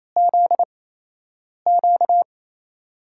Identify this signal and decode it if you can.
Morse code, heard as a keyed tone, about 28 wpm, spaced as for 8 wpm, 700 Hz. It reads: ZQ
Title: USA Call Signs - 6 Characters - 8x Character Spacing - 28wpm